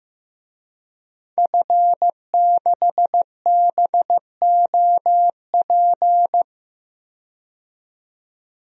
Morse code, heard as a keyed tone, about 15 wpm, 700 Hz. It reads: F6BOP
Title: DXCC-100 Call Signs - 5 Characters - 15wpm